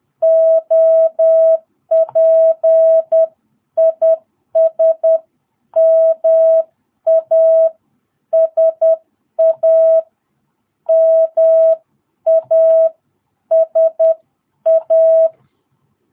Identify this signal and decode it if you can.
Morse code, heard as a keyed tone, about 10 wpm, spaced as for 9 wpm, 660 Hz. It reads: OP IS MASA MASA
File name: jh8cbh / 2022cw